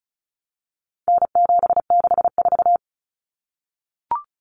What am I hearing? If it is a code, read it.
Morse code, heard as a keyed tone, about 34 wpm, 700 Hz. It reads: N764
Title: Common Cut Numbers - 4-Digit 35wpm